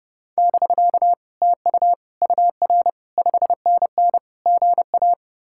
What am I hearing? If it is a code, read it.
Morse code, heard as a keyed tone, about 37 wpm, 700 Hz. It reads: <BK> TU UR 5NN GA
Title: Speed-Racing Common QSO Phrases - 30wpm